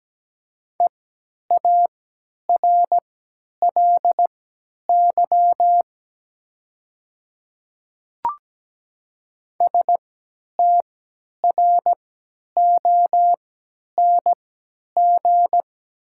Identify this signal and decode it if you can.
Morse code, heard as a keyed tone, about 17 wpm, spaced as for 10 wpm, 700 Hz. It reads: EARLY STRONG
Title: 3x Character Spacing - Top 500 Words - 17wpm